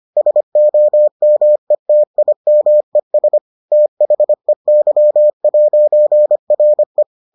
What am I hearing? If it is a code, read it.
Morse code, heard as a keyed tone, about 25 wpm, 600 Hz. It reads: SOMETIMES THEY'RE